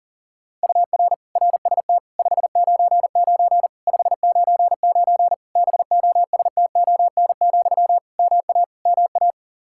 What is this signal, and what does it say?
Morse code, heard as a keyed tone, about 37 wpm, 700 Hz. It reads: UR RST 599 599 BOSTON, MA MA
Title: Practice QSO - Exchange 1 - Beginning - 40wpm